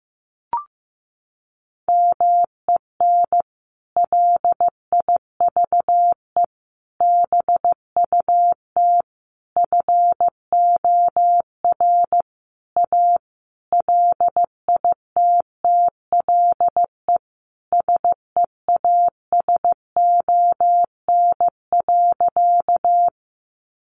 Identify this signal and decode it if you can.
Morse code, heard as a keyed tone, about 15 wpm, 700 Hz. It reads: MEN LIVE BUT FOR A LITTLE SEASON.
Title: Sentences from Top 1400 Words 15wpm